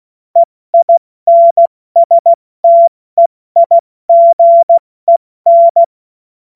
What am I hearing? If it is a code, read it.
Morse code, heard as a keyed tone, about 16 wpm, 680 Hz. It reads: EINSTEIGEN
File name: Morsekurs / Witze / Witz 20. September 2022 70 F80